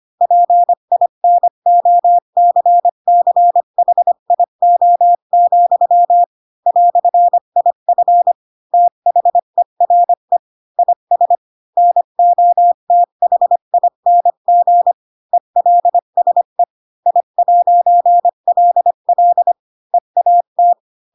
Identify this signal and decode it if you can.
Morse code, heard as a keyed tone, about 25 wpm, 700 Hz. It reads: PINOCCHIO, "IF THERE IS NOTHING ELSE I'LL EAT
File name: Pinocchio / Pinocchio ch7 20 25 ik1hge